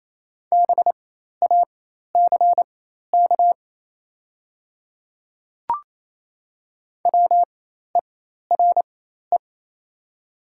Morse code, heard as a keyed tone, about 28 wpm, spaced as for 13 wpm, 690 Hz. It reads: BACK WERE
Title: 4x Character Spacing - Top 300 Words - 28wpm